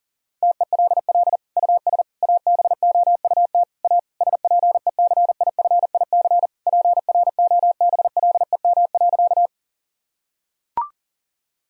Morse code, heard as a keyed tone, about 40 wpm, 700 Hz. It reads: TELL US ABOUT A SPECIFIC PROBLEM.
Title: Sentences from Top 1000 Words - Review Part 3 - 40wpm